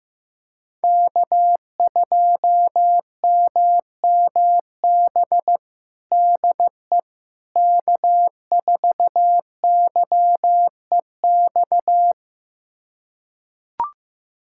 Morse code, heard as a keyed tone, about 15 wpm, 700 Hz. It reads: K2MMB DE K4YEX
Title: USA Callsign DE Callsign - 5 Characters - 15wpm